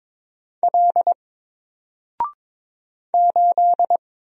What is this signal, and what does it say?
Morse code, heard as a keyed tone, about 22 wpm, 700 Hz. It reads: L 8
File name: Single-Letter-Number-22wpm